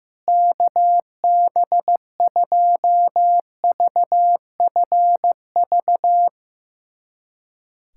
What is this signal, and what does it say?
Morse code, heard as a keyed tone, about 15 wpm, 700 Hz. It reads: KB2VFV
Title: USA Callsign DE Callsign - 6 Characters - 15wpm